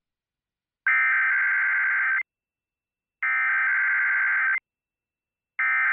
Emergency Alert System (SAME) tones (header bursts), heard as harsh buzzing bursts. It is heard from 0.9 to 5.7 s.